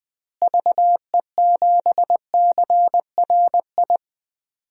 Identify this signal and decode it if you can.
Morse code, heard as a keyed tone, about 20 wpm, 700 Hz. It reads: VE7CRI
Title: DXCC-100 Call Signs - 6 Characters - Repeated 3x - 20wpm